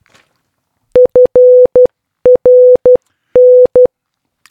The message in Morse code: FRN